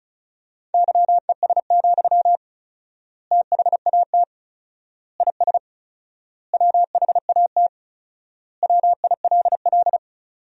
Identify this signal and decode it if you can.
Morse code, heard as a keyed tone, about 34 wpm, 700 Hz. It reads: YES, THAT IS WHAT WILL
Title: 4x Word-Spacing - Sentences from Top 500 Words - 35wpm